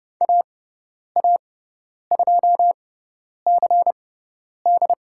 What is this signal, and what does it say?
Morse code, heard as a keyed tone, about 30 wpm, spaced as for 10 wpm, 700 Hz. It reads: AA2CD
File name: Lesson 18g - Letter m - Call Signs 30wpm